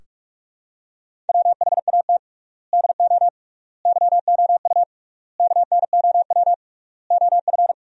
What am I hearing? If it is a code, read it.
Morse code, heard as a keyed tone, about 46 wpm, 700 Hz. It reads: WHAT DO YOU KNOW OF
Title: 3x Word-Spacing - Sentences from Top 100 Words - 45wpm